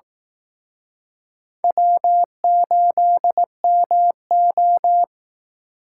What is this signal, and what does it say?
Morse code, heard as a keyed tone, about 19 wpm, 700 Hz. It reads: W8MO
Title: Speed Racing - USA 1x2 Call Signs 15wpm